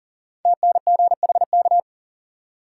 Morse code, heard as a keyed tone, about 40 wpm, 700 Hz. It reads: TNGHK
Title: Code Groups - 5 Letter:Numbers - 40wpm